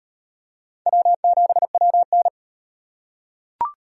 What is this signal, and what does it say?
Morse code, heard as a keyed tone, about 34 wpm, 700 Hz. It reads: W7WN